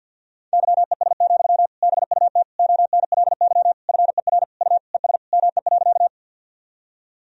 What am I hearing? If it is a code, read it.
Morse code, heard as a keyed tone, about 49 wpm, 700 Hz. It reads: YES, BUT ONLY FER U ES ME.
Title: Sentences from Top 500 Words and Abbreviations 50wpm